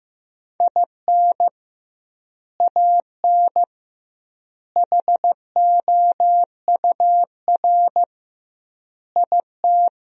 Morse code, heard as a keyed tone, about 15 wpm, 700 Hz. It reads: IN AN HOUR IT
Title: 2x Word-Spacing - Sentences from Top 400 Words - 15wpm